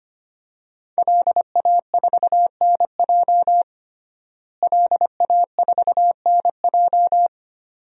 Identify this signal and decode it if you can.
Morse code, heard as a keyed tone, about 25 wpm, 700 Hz. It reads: LA4NJ LA4NJ
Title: DXCC-100 Call Signs - 5 Characters - Repeated 2x - 25wpm